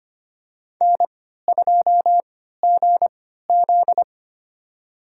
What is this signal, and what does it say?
Morse code, heard as a keyed tone, about 25 wpm, spaced as for 14 wpm, 700 Hz. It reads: N2GZ
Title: USA Call Signs - 3x Character Spacing - 25wpm